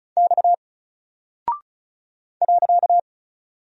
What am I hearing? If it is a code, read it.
Morse code, heard as a keyed tone, about 36 wpm, 700 Hz. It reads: X .
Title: Single Character (Letter:Number:Punctuation:Prosign) Alt 35wpm